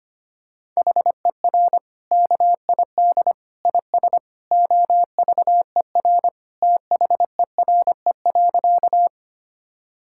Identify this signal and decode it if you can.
Morse code, heard as a keyed tone, about 25 wpm, 700 Hz. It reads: HER KID IS OVER THERE.
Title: Sentences from Top 1600 Words 25wpm